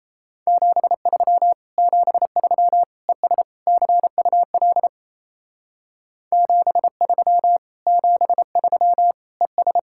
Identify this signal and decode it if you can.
Morse code, heard as a keyed tone, about 34 wpm, 700 Hz. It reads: 73 73 ES CUL 73 73 ES
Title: Speed-Racing Common QSO Phrases Encore - 28wpm